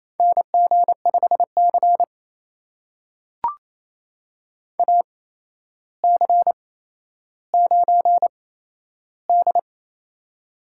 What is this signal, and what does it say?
Morse code, heard as a keyed tone, about 28 wpm, spaced as for 12 wpm, 700 Hz. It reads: NG5C AC9D
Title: USA Call Signs - 4 Characters - 8x Character Spacing - 28wpm